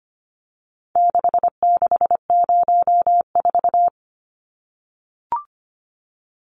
Morse code, heard as a keyed tone, about 25 wpm, 700 Hz. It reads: 6604
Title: Numbers - 4-Digit 25wpm